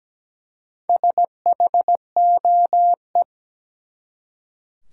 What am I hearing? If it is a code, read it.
Morse code, heard as a keyed tone, about 17 wpm, 700 Hz. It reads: SHOE
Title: Four Letter Words - 17wpm